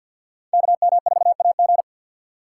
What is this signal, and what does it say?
Morse code, heard as a keyed tone, about 53 wpm, 700 Hz. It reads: KM4AG
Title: USA Call Signs - 5 Characters - 3x Character Spacing - 50wpm